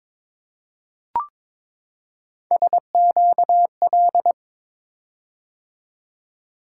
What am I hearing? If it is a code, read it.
Morse code, heard as a keyed tone, about 23 wpm, 700 Hz. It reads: SQL
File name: practice / 22wpm / Three Letter-Number Combinations - 22wpm